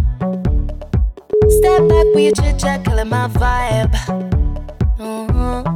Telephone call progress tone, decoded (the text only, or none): ringback tone